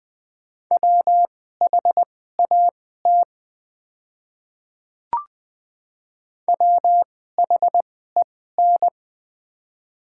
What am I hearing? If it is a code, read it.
Morse code, heard as a keyed tone, about 20 wpm, spaced as for 15 wpm, 700 Hz. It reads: WHAT WHEN
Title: 2x Character Spacing - Top 100 Words - 20wpm